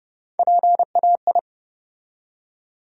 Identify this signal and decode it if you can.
Morse code, heard as a keyed tone, about 31 wpm, 700 Hz. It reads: PAI
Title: USA Call Sign Postfixes - 3 Letters 30wpm